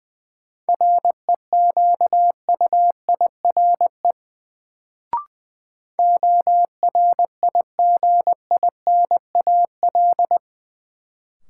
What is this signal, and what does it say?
Morse code, heard as a keyed tone, about 20 wpm, 700 Hz. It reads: REQUIRE ORIGINAL
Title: Sentences from Top 900 Words - New Word Drill - 20wpm